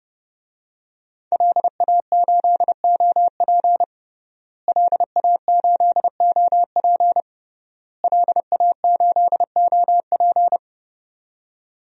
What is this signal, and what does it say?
Morse code, heard as a keyed tone, about 32 wpm, 700 Hz. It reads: LA8OP LA8OP LA8OP
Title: DXCC-100 Call Signs - 5 Characters - Repeated 3x - 30wpm